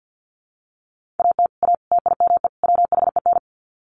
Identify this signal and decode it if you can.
Morse code, heard as a keyed tone, about 50 wpm, 700 Hz. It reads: AT A TIME WHEN
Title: Sets of 4 Words 50wpm